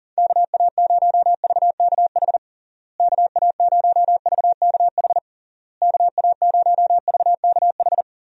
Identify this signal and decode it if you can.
Morse code, heard as a keyed tone, about 40 wpm, 700 Hz. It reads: KA0VKH KA0VKH KA0VKH
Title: USA Call Signs - 6 Characters - Repeated 3x - 40wpm